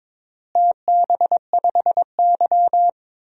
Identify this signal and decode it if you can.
Morse code, heard as a keyed tone, about 22 wpm, 700 Hz. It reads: TB5Y
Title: DXCC-100 Call Signs - 4 Characters - Repeated 3x - 22wpm